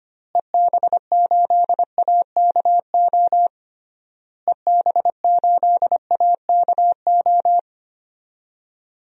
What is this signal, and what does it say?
Morse code, heard as a keyed tone, about 25 wpm, 700 Hz. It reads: EB8AKO EB8AKO
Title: DXCC-100 Call Signs - 6 Characters - Repeated 3x - 25wpm